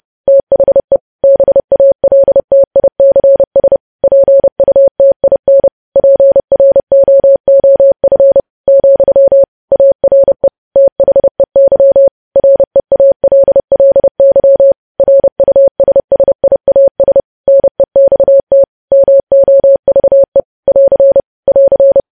Morse code, heard as a keyed tone, about 30 wpm, 550 Hz. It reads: THE BALTICS PUTIN PROOF, ARE THEY REALLY RUSSIAS NEXT MOVE <AR> <AR>
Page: news headlines in morse code at 30 wpm